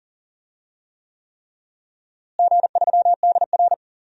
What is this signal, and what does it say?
Morse code, heard as a keyed tone, about 40 wpm, 700 Hz. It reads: G3DR
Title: DXCC-100 Call Signs - 4 Characters - Repeated 3x - 40wpm